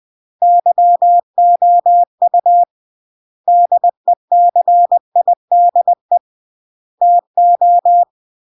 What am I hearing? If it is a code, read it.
Morse code, heard as a keyed tone, about 20 wpm, 700 Hz. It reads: YOU DECIDE TO
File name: CW / jodorowsky / 82 commandments 20-20